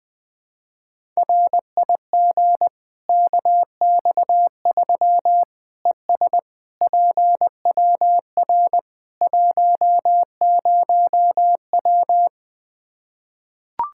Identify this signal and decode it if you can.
Morse code, heard as a keyed tone, about 20 wpm, 700 Hz. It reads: RIG KX3 ES PWR 10W
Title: Practice QSO - Exchange 2 - Info - 20wpm